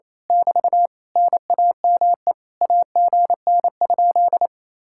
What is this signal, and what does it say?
Morse code, heard as a keyed tone, about 27 wpm, 700 Hz. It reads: <BT> NAME AGN?
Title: Speed-Racing Common QSO Phrases - 28wpm